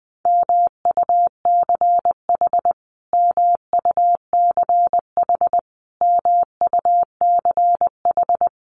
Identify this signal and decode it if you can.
Morse code, heard as a keyed tone, about 20 wpm, 700 Hz. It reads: MUCH MUCH MUCH
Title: Top 300 Words Repeated 20wpm